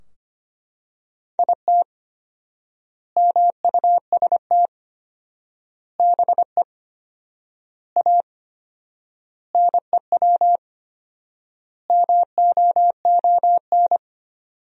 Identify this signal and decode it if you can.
Morse code, heard as a keyed tone, about 25 wpm, 700 Hz. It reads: IT MUST BE A NEW MOON
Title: 4x Word-Spacing - Sentences from Top 500 Words - 25wpm